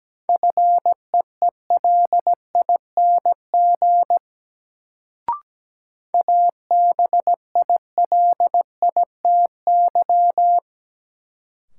Morse code, heard as a keyed tone, about 17 wpm, 700 Hz. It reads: FEELING ABILITY
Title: Sentences from Top 1200 Words - New Word Drill Alt - 17wpm